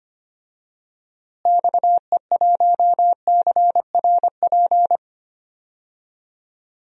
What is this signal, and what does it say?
Morse code, heard as a keyed tone, about 25 wpm, 700 Hz. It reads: XE1CRP
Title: DXCC-100 Call Signs - 6 Characters - Repeated 3x - 25wpm